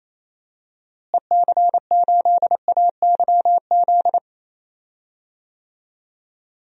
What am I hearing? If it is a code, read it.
Morse code, heard as a keyed tone, about 28 wpm, 700 Hz. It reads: EC8AYZ
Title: DXCC-100 Call Signs - 6 Characters - Repeated 3x - 28wpm